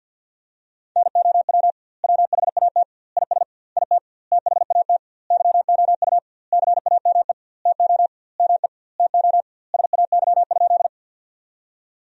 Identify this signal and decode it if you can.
Morse code, heard as a keyed tone, about 47 wpm, 700 Hz. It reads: NOW WHAT IS IT THAT YOU CAME TO ME TO SAY?
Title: 2x Word-Spacing - Sentences from Top 200 Words - 50wpm